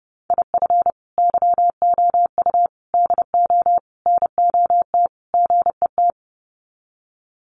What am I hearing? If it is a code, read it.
Morse code, heard as a keyed tone, about 30 wpm, 700 Hz. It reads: IF YOU DO NOT GET
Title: Sets of 5 Words Encore2 30wpm